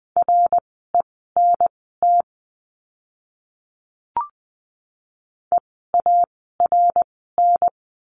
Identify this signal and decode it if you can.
Morse code, heard as a keyed tone, about 20 wpm, spaced as for 14 wpm, 700 Hz. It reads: RENT EARN
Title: 054 Lesson 10f - Letter r - Words 20wpm